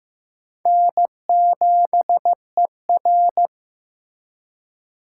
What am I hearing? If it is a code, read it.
Morse code, heard as a keyed tone, about 15 wpm, 700 Hz. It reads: N7ER
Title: USA Call Signs - 4 Characters - 3x Character Spacing - 15wpm